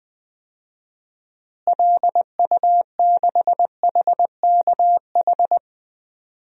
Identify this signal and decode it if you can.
Morse code, heard as a keyed tone, about 21 wpm, 700 Hz. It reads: LU6HKH
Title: DXCC-100 Call Signs - 6 Characters - Repeated 3x - 20wpm